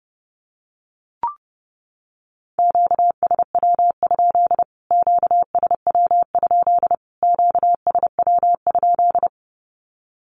Morse code, heard as a keyed tone, about 30 wpm, 700 Hz. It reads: QSW? QSW? QSW?
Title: Q-Codes Repeated 3x 30wpm